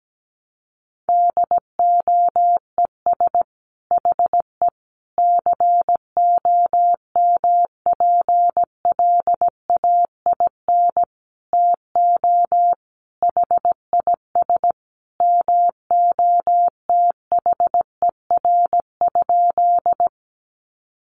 Morse code, heard as a keyed tone, about 17 wpm, 700 Hz. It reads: DOES HE COMPLAIN TO HIS MOTHER?
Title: Sentences from Top 1300 Words 17wpm